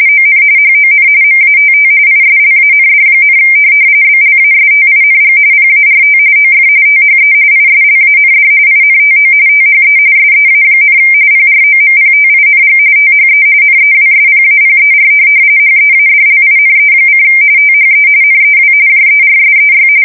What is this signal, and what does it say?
RTTY, about 45.45 baud. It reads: ON HON MATT ISLAND, BLASTING PART OF THE ISLAND INTO THE SEA, PROMPTING THE NEWSPAPER HEADLINE: THE NEW JERSEY
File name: fsk170 / USS-New-Jersey